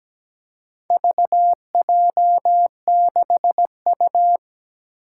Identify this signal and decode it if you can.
Morse code, heard as a keyed tone, about 17 wpm, 700 Hz. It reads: VJ6U
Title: Four Challenging Letter-Number Combinations 17wpm